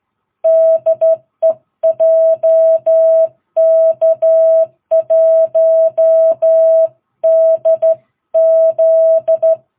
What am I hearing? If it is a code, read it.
Morse code, heard as a keyed tone, about 12 wpm, 650 Hz. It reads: DE JK1DZ
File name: jh8cbh / 2022cw